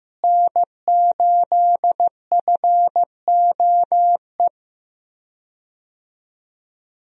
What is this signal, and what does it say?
Morse code, heard as a keyed tone, about 16 wpm, 700 Hz. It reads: N8FOE
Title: USA Call Signs - 5 Characters - Repeated 2x - 15wpm